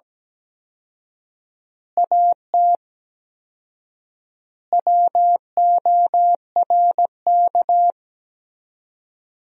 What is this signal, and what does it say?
Morse code, heard as a keyed tone, about 17 wpm, 700 Hz. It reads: AT WORK
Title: 4x Word-Spacing - Sentences from Top 200 Words - 17wpm